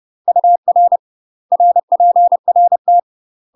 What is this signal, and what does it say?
Morse code, heard as a keyed tone, about 30 wpm, 700 Hz. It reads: UR RPRT
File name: UK MASTER QSOs TRAINING / 30WPM / UK MASTER QSOs 0001